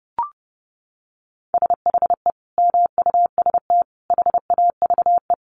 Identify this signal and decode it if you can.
Morse code, heard as a keyed tone, about 30 wpm, 700 Hz. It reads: SHE MUST HAVE
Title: Sentences from Top 1100 Words 30wpm